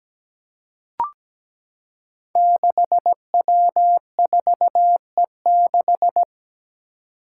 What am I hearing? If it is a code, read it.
Morse code, heard as a keyed tone, about 17 wpm, 700 Hz. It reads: 6W4E6
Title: Five Challenging Letter-Number Combinations 17wpm